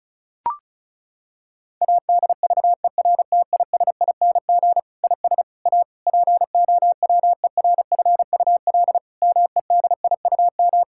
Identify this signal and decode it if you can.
Morse code, heard as a keyed tone, about 34 wpm, 700 Hz. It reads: ADVERTISING IS A POWERFUL MEDIUM
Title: Sentences from Top 1000 Words - Review Part 6 - 35wpm